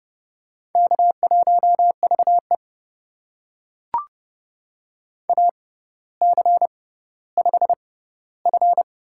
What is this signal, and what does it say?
Morse code, heard as a keyed tone, about 30 wpm, spaced as for 15 wpm, 700 Hz. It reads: K1VE AC5F
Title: USA Call Signs - 4 Characters - 6x Character Spacing - 30wpm